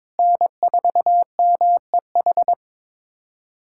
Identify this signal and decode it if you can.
Morse code, heard as a keyed tone, about 22 wpm, 700 Hz. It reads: N4MEH
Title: USA Callsign DE Callsign - 5 Characters - 22wpm